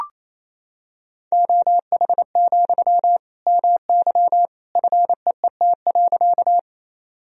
Morse code, heard as a keyed tone, about 28 wpm, 700 Hz. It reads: OH, MY FEET.
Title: Sentences from Top 1000 Words - Review Part 1 - 28wpm